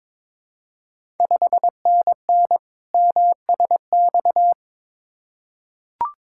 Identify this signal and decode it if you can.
Morse code, heard as a keyed tone, about 22 wpm, 700 Hz. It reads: 5NN MSX
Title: Ontario QSO Party Exchange 22wpm